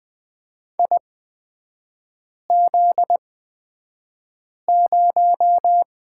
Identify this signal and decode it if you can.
Morse code, heard as a keyed tone, about 20 wpm, spaced as for 5 wpm, 700 Hz. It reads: IZ0
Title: 243 Lesson 39e - Letter z - Three Characters 20wpm